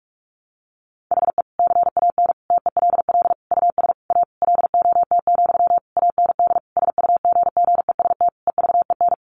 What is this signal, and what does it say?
Morse code, heard as a keyed tone, about 44 wpm, 700 Hz. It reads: HE CAN TELL US A LOT, AND SUGGEST EVEN
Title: Sentences from Top 1300 Words 45wpm